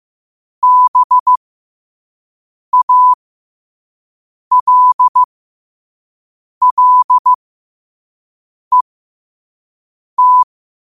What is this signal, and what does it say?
Morse code, heard as a keyed tone, about 15 wpm, spaced as for 5 wpm, 1000 Hz. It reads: BALLET